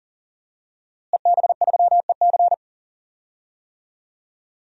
Morse code, heard as a keyed tone, about 40 wpm, 700 Hz. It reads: EB3EC
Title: DXCC-100 Call Signs - 5 Characters - Repeated 2x - 40wpm